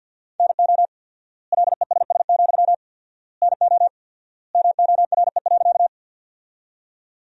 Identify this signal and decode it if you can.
Morse code, heard as a keyed tone, about 50 wpm, 700 Hz. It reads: NO LESS, NO MORE.
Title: 4x Word-Spacing - Sentences from Top 400 Words - 50wpm